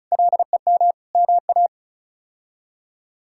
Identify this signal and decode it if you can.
Morse code, heard as a keyed tone, about 36 wpm, 700 Hz. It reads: LEM MA
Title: K1USN SST QSO Practice - 35wpm